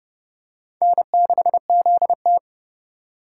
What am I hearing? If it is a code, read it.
Morse code, heard as a keyed tone, about 29 wpm, 700 Hz. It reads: N6ZT